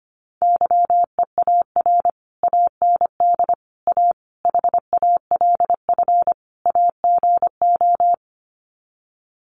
Morse code, heard as a keyed tone, about 25 wpm, 700 Hz. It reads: YEAR AND A HALF AGO
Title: Sets of 5 Words Encore 25wpm